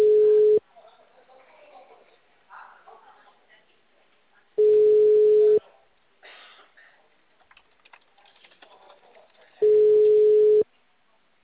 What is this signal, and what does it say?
Ringback tone (Europe, 425 Hz), with speech.